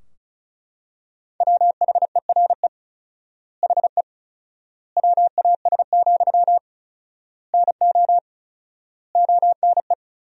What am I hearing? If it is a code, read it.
Morse code, heard as a keyed tone, about 34 wpm, 700 Hz. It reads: WHERE HE WAS, NO ONE
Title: 4x Word-Spacing - Sentences from Top 400 Words - 35wpm